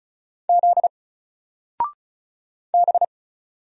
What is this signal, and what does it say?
Morse code, heard as a keyed tone, about 35 wpm, 700 Hz. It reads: Z B
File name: Single-Letter-Number-35wpm